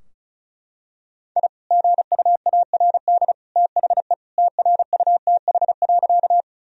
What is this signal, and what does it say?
Morse code, heard as a keyed tone, about 36 wpm, 700 Hz. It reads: I GUARD THE TRUTH.
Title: Sentences from Top 1000 Words - Review Part 4 - 35wpm